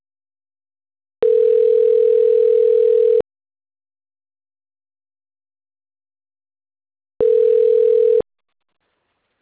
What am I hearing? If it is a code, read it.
Ringback tone (North America), 440 + 480 Hz.